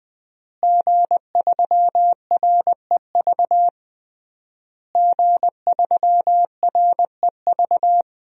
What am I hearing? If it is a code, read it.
Morse code, heard as a keyed tone, about 20 wpm, 700 Hz. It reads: G3REV G3REV
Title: DXCC-100 Call Signs - 5 Characters - Repeated 3x - 20wpm